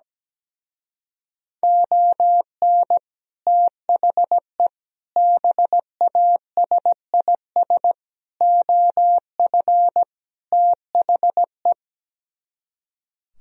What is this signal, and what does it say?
Morse code, heard as a keyed tone, about 17 wpm, 700 Hz. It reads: ON THE BASIS OF THE
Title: Speed-Racing - Sets of 5 Words 17wpm